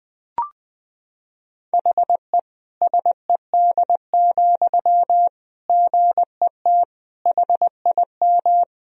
Morse code, heard as a keyed tone, about 21 wpm, 700 Hz. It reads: HE SED, GET HIM
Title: Sentences from Top 400 Words and Abbreviations 20wpm